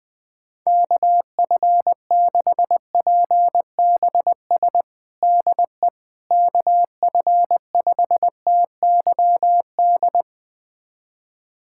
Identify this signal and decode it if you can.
Morse code, heard as a keyed tone, about 21 wpm, 700 Hz. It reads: KF6PBS DE KF5TYD
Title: USA Callsign DE Callsign - 6 Characters - 20wpm